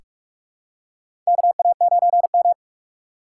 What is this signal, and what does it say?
Morse code, heard as a keyed tone, about 44 wpm, 700 Hz. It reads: KA9M
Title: USA Call Signs - 4 Characters - 6x Character Spacing - 45wpm